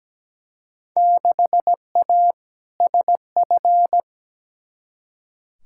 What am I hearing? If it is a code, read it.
Morse code, heard as a keyed tone, about 17 wpm, 700 Hz. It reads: 6A SF
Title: Field Day Exchanges - 17wpm